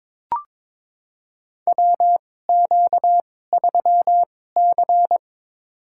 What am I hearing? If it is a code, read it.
Morse code, heard as a keyed tone, about 22 wpm, spaced as for 16 wpm, 700 Hz. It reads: WQ3C
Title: USA Call Signs - 2x Character Spacing - 22wpm